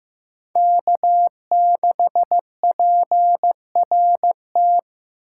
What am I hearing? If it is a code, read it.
Morse code, heard as a keyed tone, about 18 wpm, 700 Hz. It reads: K6PRT
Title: Speed Racing - USA 1x3 Call Signs 15wpm